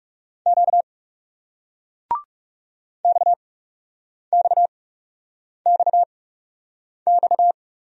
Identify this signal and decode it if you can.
Morse code, heard as a keyed tone, about 39 wpm, spaced as for 8 wpm, 700 Hz. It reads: Q XXXX